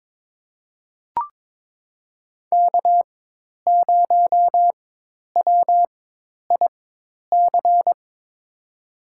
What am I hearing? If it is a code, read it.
Morse code, heard as a keyed tone, about 22 wpm, spaced as for 10 wpm, 700 Hz. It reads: K0WIC
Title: USA Call Signs - 5 Characters - 4x Character Spacing - 22wpm